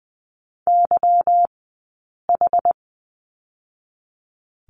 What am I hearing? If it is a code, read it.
Morse code, heard as a keyed tone, about 20 wpm, 700 Hz. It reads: Y H
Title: 158 Lesson 26d - Letter g - Two Characters 20wpm